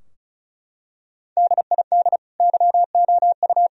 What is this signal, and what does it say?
Morse code, heard as a keyed tone, about 34 wpm, 700 Hz. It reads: DID YOU
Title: Sentences from Top 1000 Words - Review Part 2 - 35wpm